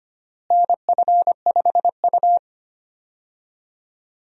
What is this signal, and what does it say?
Morse code, heard as a keyed tone, about 25 wpm, 700 Hz. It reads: NF5U
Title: USA Call Signs - 4 Characters - Repeated 3x - 25wpm